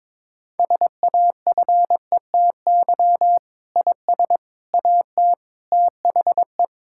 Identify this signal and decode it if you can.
Morse code, heard as a keyed tone, about 22 wpm, 700 Hz. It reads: SAFETY IS AT THE
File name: Sentences with 7-Word Phrases - 22wpm